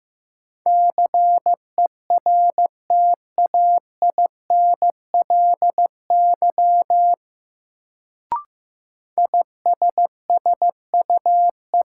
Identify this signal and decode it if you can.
Morse code, heard as a keyed tone, about 15 wpm, 700 Hz. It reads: CERTAINLY ISSUE
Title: Sentences from Top 900 Words - New Word Drill - 15wpm